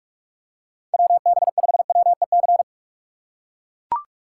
Morse code, heard as a keyed tone, about 42 wpm, 700 Hz. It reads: WB5WEC
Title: K1USN SST QSO Practice - 45wpm